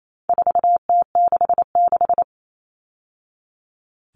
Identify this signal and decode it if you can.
Morse code, heard as a keyed tone, about 28 wpm, 700 Hz. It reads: 4T66
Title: All Cut Numbers - 4-Digit 28wpm